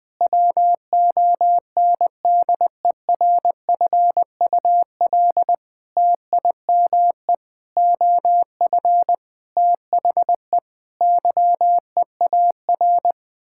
Morse code, heard as a keyed tone, about 20 wpm, 700 Hz. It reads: WONDERFUL TIME OF THE YEAR
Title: Sets of 8 Words - 20wpm